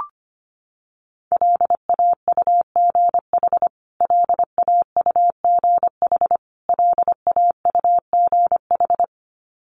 Morse code, heard as a keyed tone, about 25 wpm, 700 Hz. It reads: LAUGH LAUGH LAUGH
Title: Top 500 Words Repeated 25wpm